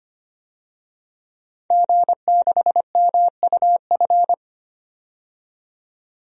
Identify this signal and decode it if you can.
Morse code, heard as a keyed tone, about 25 wpm, 700 Hz. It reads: G6MUF
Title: DXCC-100 Call Signs - 5 Characters - 25wpm